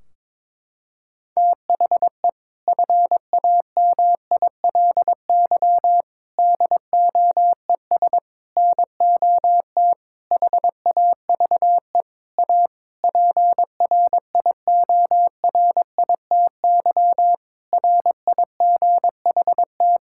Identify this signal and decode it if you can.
Morse code, heard as a keyed tone, about 22 wpm, 700 Hz. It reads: THE FAMILY DOES NOT HAVE A PRIORITY RIGHT
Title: Sentences from Top 1600 Words 22wpm